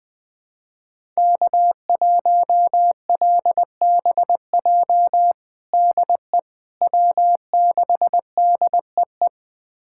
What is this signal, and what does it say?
Morse code, heard as a keyed tone, about 21 wpm, 700 Hz. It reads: K1LBJ DE W6DEE
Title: USA Callsign DE Callsign - 5 Characters - 20wpm